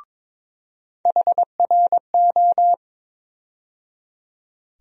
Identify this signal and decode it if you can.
Morse code, heard as a keyed tone, about 22 wpm, 700 Hz. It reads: HRO
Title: USA Call Sign Postfixes - 3 Letters 22wpm